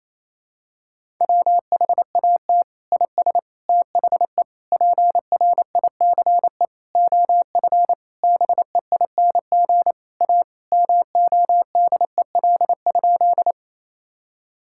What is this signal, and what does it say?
Morse code, heard as a keyed tone, about 28 wpm, 700 Hz. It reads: WHAT IS THE PRICE OF BEING A MODEL?
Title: Sentences from Top 1000 Words - Review Part 5 - 28wpm